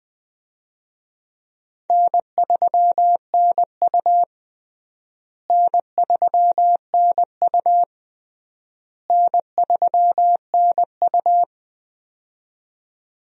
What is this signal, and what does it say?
Morse code, heard as a keyed tone, about 21 wpm, 700 Hz. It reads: N3NU N3NU N3NU
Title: USA Call Signs - 4 Characters - Repeated 3x - 20wpm